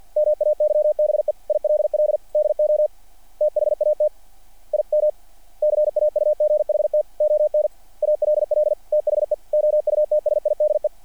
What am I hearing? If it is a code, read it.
Morse code, heard as a keyed tone, about 48 wpm, 590 Hz. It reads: MAYBE ILL DO THAT IM CAUGHT ON ALL THE OUTSIDE
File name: archived / CRSnet